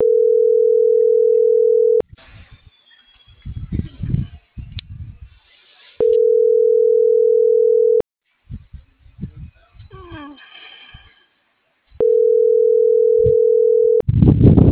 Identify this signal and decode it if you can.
Ringback tone (North America), 440 + 480 Hz.